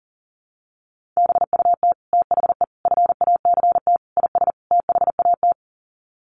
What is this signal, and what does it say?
Morse code, heard as a keyed tone, about 40 wpm, 700 Hz. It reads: BUT THE FACT IS THAT
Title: Sets of 5 Words Encore2 40wpm